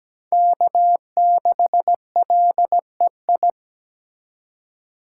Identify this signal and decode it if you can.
Morse code, heard as a keyed tone, about 17 wpm, 700 Hz. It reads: K6LEI
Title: USA Call Signs - 2x Character Spacing - 17wpm